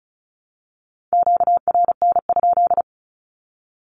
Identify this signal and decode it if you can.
Morse code, heard as a keyed tone, about 34 wpm, 700 Hz. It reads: QRN?
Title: Q-Codes Repeated 3x 35wpm